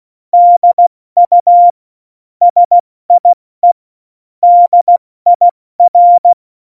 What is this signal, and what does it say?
Morse code, heard as a keyed tone, about 16 wpm, 700 Hz. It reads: DU SIE DIR